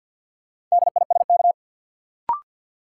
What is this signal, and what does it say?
Morse code, heard as a keyed tone, about 52 wpm, 700 Hz. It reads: DISK
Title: Sentences from Top 1000 Words - Review Part 6 - New Word Drill - 50wpm